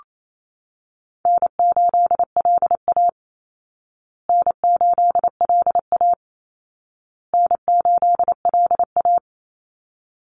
Morse code, heard as a keyed tone, about 28 wpm, 700 Hz. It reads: N8LA N8LA N8LA
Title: Easy Call Signs Repeated 3x - 28wpm